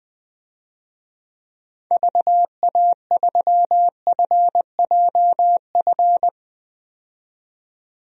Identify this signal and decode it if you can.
Morse code, heard as a keyed tone, about 20 wpm, 700 Hz. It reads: VA3FJF
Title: DXCC-100 Call Signs - 6 Characters - 20wpm